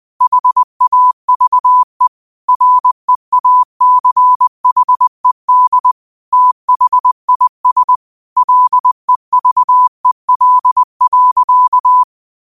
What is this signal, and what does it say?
Morse code, heard as a keyed tone, about 20 wpm, 1000 Hz. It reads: HAVE REACHED THIS LEVEL.
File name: qotdfiles / 20wpm